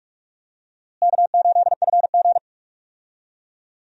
Morse code, heard as a keyed tone, about 46 wpm, 700 Hz. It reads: K8FG
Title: USA Call Signs - 4 Characters - Repeated 3x - 45wpm